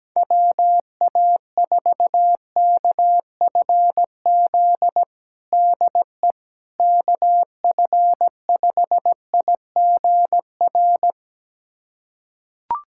Morse code, heard as a keyed tone, about 17 wpm, 700 Hz. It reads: WA4KFZ DE KF5IGR
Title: USA Callsign DE Callsign - 6 Characters - 17wpm